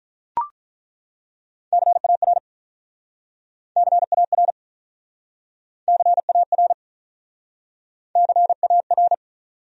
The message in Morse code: CAR CAR CAR CAR

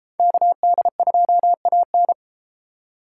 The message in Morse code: KD2AN

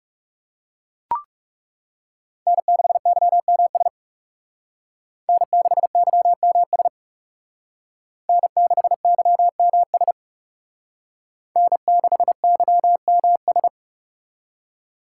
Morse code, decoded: N6YMS N6YMS N6YMS N6YMS

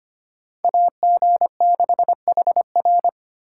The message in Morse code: AG6HR

